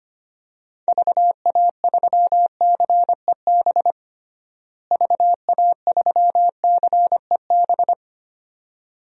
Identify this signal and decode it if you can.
Morse code, heard as a keyed tone, about 25 wpm, 700 Hz. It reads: VA3CEB VA3CEB
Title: DXCC-100 Call Signs - 6 Characters - Repeated 2x - 25wpm